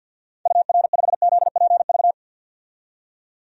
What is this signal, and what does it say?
Morse code, heard as a keyed tone, about 47 wpm, 700 Hz. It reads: UR5ZPV
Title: DXCC-100 Call Signs - 6 Characters - 50wpm